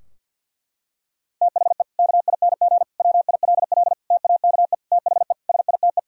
Morse code, heard as a keyed tone, about 49 wpm, 700 Hz. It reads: THE KING WILL TAKE THE SITE
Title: Sentences from Top 1000 Words - Review Part 4 - 50wpm